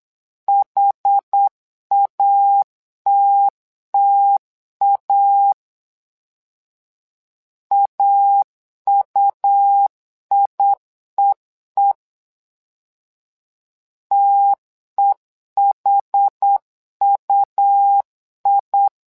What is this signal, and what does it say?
Morse code, heard as a keyed tone, about 8 wpm, 800 Hz. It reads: HATTA AUIEE TEHUI
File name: LEZIONI / 2 UHDN / 2 UH 8wpm 2129